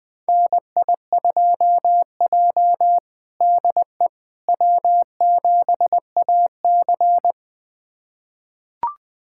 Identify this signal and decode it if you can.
Morse code, heard as a keyed tone, about 20 wpm, 700 Hz. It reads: NI2J DE W7AC
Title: USA Callsign DE Callsign - 4 Characters - 20wpm